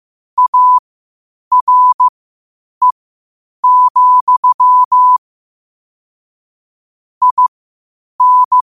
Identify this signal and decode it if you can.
Morse code, heard as a keyed tone, about 15 wpm, spaced as for 8 wpm, 1000 Hz. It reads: ARE, IN
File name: qotdfiles / 8wpm